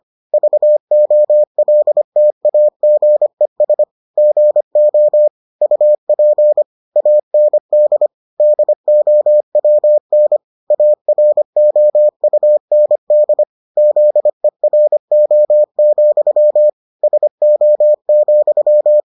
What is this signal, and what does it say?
Morse code, heard as a keyed tone, about 25 wpm, 600 Hz. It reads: VOLTAGES GO UP AND DOWN AROUND ZERO, SO,